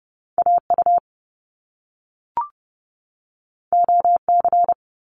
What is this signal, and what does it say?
Morse code, heard as a keyed tone, about 30 wpm, 700 Hz. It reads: AU OC
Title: Top Two Letter Combinations in Words 30wpm